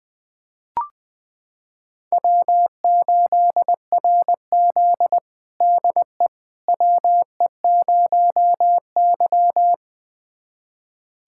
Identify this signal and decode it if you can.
Morse code, heard as a keyed tone, about 21 wpm, 700 Hz. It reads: W8RZ DE WE0Y
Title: USA Callsign DE Callsign - 4 Characters - 20wpm